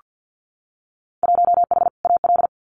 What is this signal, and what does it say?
Morse code, heard as a keyed tone, about 48 wpm, 700 Hz. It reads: 1H AL
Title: Winter Field Day Exchanges 50wpm